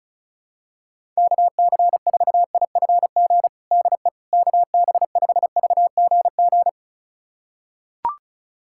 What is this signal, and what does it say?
Morse code, heard as a keyed tone, about 36 wpm, 700 Hz. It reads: KC4IFG DE KB5VGG
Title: USA Callsign DE Callsign - 6 Characters - 35wpm